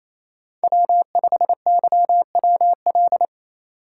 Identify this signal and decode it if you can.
Morse code, heard as a keyed tone, about 35 wpm, 700 Hz. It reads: W5YWL
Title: Speed Racing - USA 1x3 Call Signs 28wpm